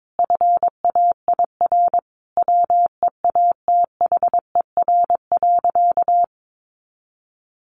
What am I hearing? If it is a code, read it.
Morse code, heard as a keyed tone, about 22 wpm, 700 Hz. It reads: FAIR WEATHER.
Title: Sentences from Top 1300 Words 22wpm